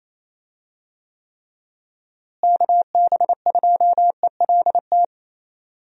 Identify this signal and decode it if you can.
Morse code, heard as a keyed tone, about 27 wpm, 700 Hz. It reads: KB2ELT
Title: USA Call Signs - 6 Characters - Repeated 2x - 28wpm